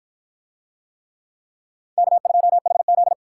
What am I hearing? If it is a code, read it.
Morse code, heard as a keyed tone, about 53 wpm, 700 Hz. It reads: K2HZ